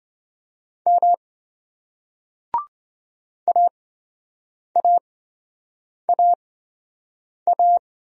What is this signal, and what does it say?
Morse code, heard as a keyed tone, about 26 wpm, spaced as for 7 wpm, 700 Hz. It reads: M AAAA